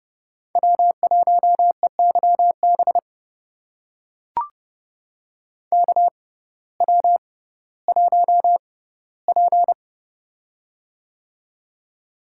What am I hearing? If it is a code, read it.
Morse code, heard as a keyed tone, about 30 wpm, spaced as for 17 wpm, 700 Hz. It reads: W1EYB KW1P
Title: USA Call Signs - 6x Character Spacing - 30wpm